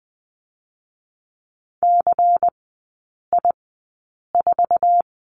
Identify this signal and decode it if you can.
Morse code, heard as a keyed tone, about 20 wpm, spaced as for 8 wpm, 690 Hz. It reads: CI4
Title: 191 Lesson 31e - Letter v - Three Characters 20wpm